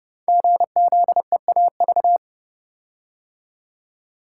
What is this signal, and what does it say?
Morse code, heard as a keyed tone, about 29 wpm, 700 Hz. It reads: GZEAV
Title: Code Groups - 5 Letter:Numbers - 30wpm